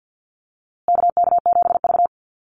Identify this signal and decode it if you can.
Morse code, heard as a keyed tone, about 50 wpm, 700 Hz. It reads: KK7V
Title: USA Call Signs Repeated 2x 50wpm